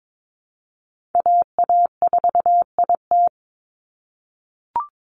Morse code, heard as a keyed tone, about 22 wpm, 700 Hz. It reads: AA4IT
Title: NA SOTA Chaser Callsigns 22wpm